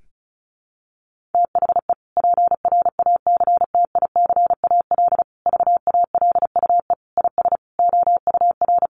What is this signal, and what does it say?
Morse code, heard as a keyed tone, about 34 wpm, 700 Hz. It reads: THE PRACTICAL VALUE IS OUR
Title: Sentences from Top 1400 Words 35wpm